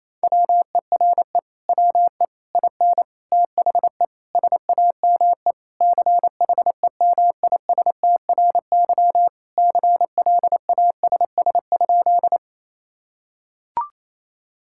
Morse code, heard as a keyed tone, about 28 wpm, 700 Hz. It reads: WERE WE IN THE SAME CHEMISTRY CLASS?
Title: Sentences from Top 1000 Words - Review Part 6 - 28wpm